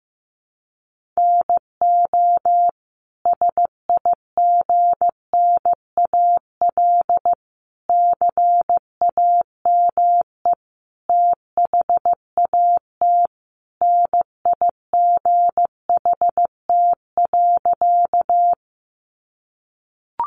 Morse code, heard as a keyed tone, about 15 wpm, 700 Hz. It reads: NO SIGNAL CAME THAT NIGHT.
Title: Sentences from Top 1400 Words 15wpm